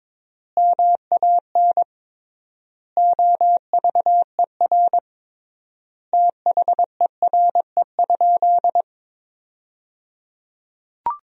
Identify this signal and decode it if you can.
Morse code, heard as a keyed tone, about 22 wpm, 700 Hz. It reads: MAN OVER THERE?
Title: 3x Word-Spacing - Sentences from Top 200 Words - 22wpm